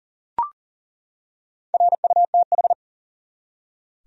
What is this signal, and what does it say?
Morse code, heard as a keyed tone, about 40 wpm, 700 Hz. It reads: RUTH
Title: USA Common Names - Max 5 Letters - 40wpm